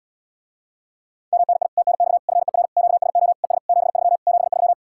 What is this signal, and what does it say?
Morse code, heard as a keyed tone, about 77 wpm, 700 Hz. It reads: MNI TNX FER QSO ES 73 73